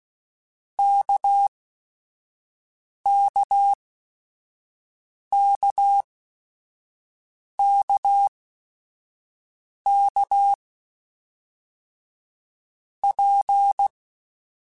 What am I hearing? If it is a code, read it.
Morse code, heard as a keyed tone, about 16 wpm, 800 Hz. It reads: K K K K K P
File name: Basic Sets / Basic Letter Groups Collective7 / RKPX@5